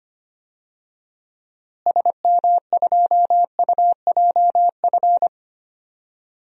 Morse code, heard as a keyed tone, about 25 wpm, 700 Hz. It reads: SM2UJF
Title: DXCC-100 Call Signs - 6 Characters - Repeated 2x - 25wpm